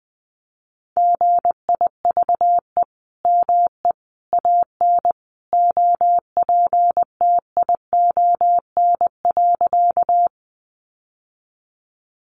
Morse code, heard as a keyed tone, about 20 wpm, 700 Hz. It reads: GIVE ME AN OPTION.
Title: Sentences from Top 1300 Words 20wpm